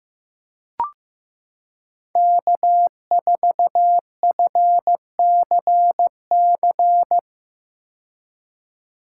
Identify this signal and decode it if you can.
Morse code, heard as a keyed tone, about 15 wpm, 700 Hz. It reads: K4FCC
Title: K1USN SST QSO Practice - 15wpm